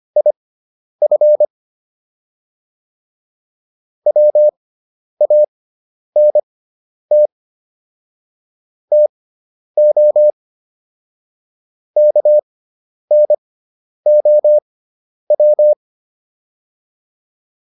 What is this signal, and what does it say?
Morse code, heard as a keyed tone, about 25 wpm, spaced as for 10 wpm, 600 Hz. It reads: IF WANT TO KNOW